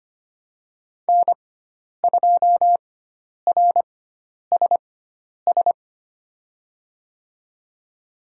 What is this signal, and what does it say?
Morse code, heard as a keyed tone, about 25 wpm, spaced as for 10 wpm, 700 Hz. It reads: N2RSS
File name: Lesson 16g - Letter c - Call Signs 25wpm